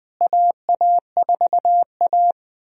Morse code, heard as a keyed tone, about 20 wpm, 700 Hz. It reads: AA4A
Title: USA Call Signs - 4 Characters - Repeated 2x - 20wpm